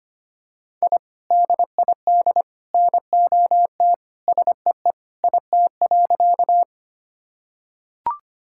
Morse code, heard as a keyed tone, about 25 wpm, 700 Hz. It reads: I DID NOT SEE IT.
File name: practice / 25wpm / Sentences with 3-Word Phrases - 25wpm